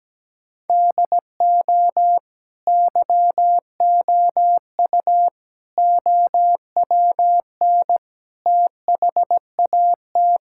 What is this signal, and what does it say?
Morse code, heard as a keyed tone, about 17 wpm, 700 Hz. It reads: DO YOU OWN THAT
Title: Sentences from Top 1000 Words - Review Part 6 - 17wpm